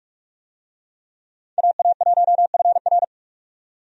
Morse code, heard as a keyed tone, about 44 wpm, 700 Hz. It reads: AA1FR
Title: USA Call Signs - 5 Characters - 45wpm